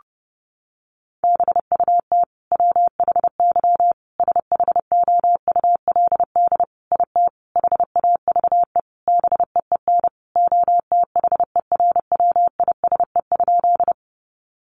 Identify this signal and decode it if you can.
Morse code, heard as a keyed tone, about 30 wpm, 700 Hz. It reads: BUT WHY SHOULD IT HAVE BEEN OTHERWISE?
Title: Sentences from Top 1200 Words 30wpm